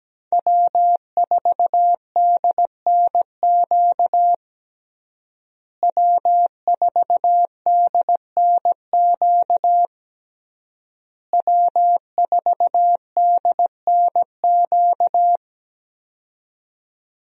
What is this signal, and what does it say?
Morse code, heard as a keyed tone, about 17 wpm, 700 Hz. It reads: W4DNQ W4DNQ W4DNQ
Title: USA Call Signs - 5 Characters - Repeated 3x - 17wpm